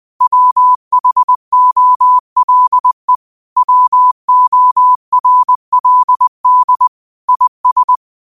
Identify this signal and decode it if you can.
Morse code, heard as a keyed tone, about 20 wpm, 1000 Hz. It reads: WHOLE WORLD IS
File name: qotdfiles / 20wpm